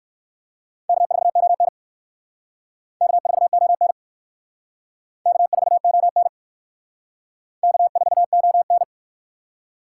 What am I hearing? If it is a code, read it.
Morse code, heard as a keyed tone, about 59 wpm, 700 Hz. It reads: K4ON K4ON K4ON K4ON